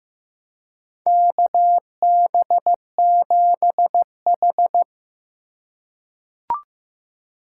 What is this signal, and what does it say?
Morse code, heard as a keyed tone, about 15 wpm, 700 Hz. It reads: KB7H
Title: K1USN SST QSO Practice - 15wpm